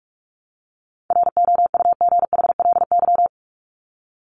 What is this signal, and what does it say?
Morse code, heard as a keyed tone, about 44 wpm, 700 Hz. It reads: ROUGHLY
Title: Top 1251-1500 Words 45wpm